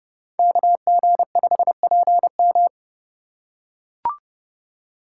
Morse code, heard as a keyed tone, about 27 wpm, 700 Hz. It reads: KG5PM